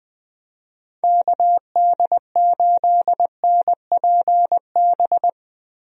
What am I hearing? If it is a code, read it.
Morse code, heard as a keyed tone, about 21 wpm, 700 Hz. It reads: KD8NPB
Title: USA Call Signs - 6 Characters - Repeated 3x - 20wpm